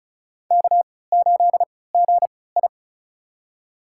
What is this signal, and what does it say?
Morse code, heard as a keyed tone, about 36 wpm, spaced as for 20 wpm, 700 Hz. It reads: K8GI
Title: USA Call Signs - 4 Characters - 3x Character Spacing - 35wpm